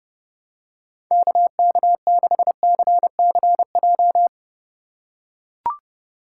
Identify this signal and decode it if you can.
Morse code, heard as a keyed tone, about 31 wpm, 700 Hz. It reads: KK6CCJ
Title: USA Call Signs - 6 Characters - 3x Character Spacing - 30wpm